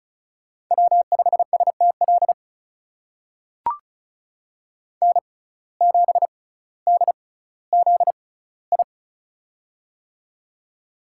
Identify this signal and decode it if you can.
Morse code, heard as a keyed tone, about 36 wpm, spaced as for 18 wpm, 700 Hz. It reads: W5STL N7DZI
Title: USA Call Signs - 5 Characters - 6x Character Spacing - 35wpm